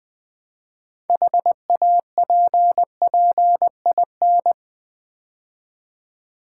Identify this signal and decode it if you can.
Morse code, heard as a keyed tone, about 20 wpm, 700 Hz. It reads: HAPPIN
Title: Build A Word - 3 Letters At A Time - 9-Letter Words - 20wpm